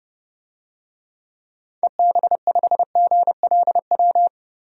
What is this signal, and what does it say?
Morse code, heard as a keyed tone, about 30 wpm, 700 Hz. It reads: EB5GLW
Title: DXCC-100 Call Signs - 6 Characters - Repeated 3x - 30wpm